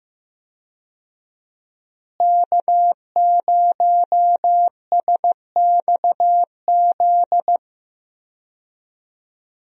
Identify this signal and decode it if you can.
Morse code, heard as a keyed tone, about 16 wpm, 700 Hz. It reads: K0SXZ
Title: USA Call Signs - 5 Characters - Repeated 3x - 15wpm